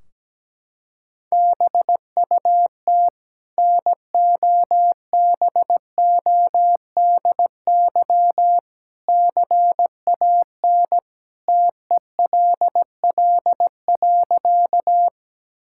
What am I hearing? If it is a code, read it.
Morse code, heard as a keyed tone, about 17 wpm, 700 Hz. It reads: BUT NOBODY CAN TELL.
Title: Sentences from Top 1600 Words 17wpm